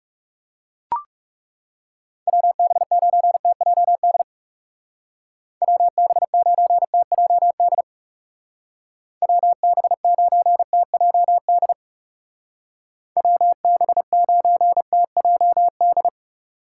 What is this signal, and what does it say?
Morse code, heard as a keyed tone, about 36 wpm, 700 Hz. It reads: WB9TJD WB9TJD WB9TJD WB9TJD
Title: Speed Racing - USA Call Signs 30wpm